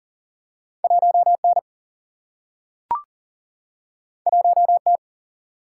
Morse code, heard as a keyed tone, about 40 wpm, 700 Hz. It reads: 1 N 1 T
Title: Single-Letter Single-Number 40wpm